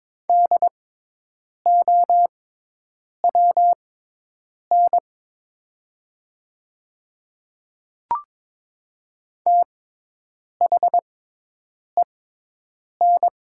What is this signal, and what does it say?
Morse code, heard as a keyed tone, about 22 wpm, spaced as for 8 wpm, 700 Hz. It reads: DOWN THEN
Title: 6x Character Spacing - Top 300 Words - 22wpm